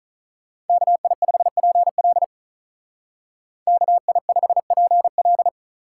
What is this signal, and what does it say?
Morse code, heard as a keyed tone, about 44 wpm, 700 Hz. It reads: KI5PL KI5PL